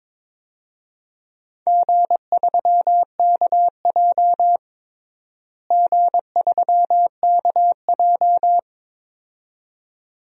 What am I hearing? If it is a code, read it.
Morse code, heard as a keyed tone, about 22 wpm, 700 Hz. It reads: G3KJ G3KJ
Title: DXCC-100 Call Signs - 4 Characters - Repeated 2x - 22wpm